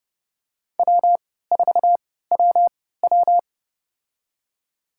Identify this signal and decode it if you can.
Morse code, heard as a keyed tone, about 30 wpm, spaced as for 17 wpm, 700 Hz. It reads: W4WW
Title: USA Call Signs - 3x Character Spacing - 30wpm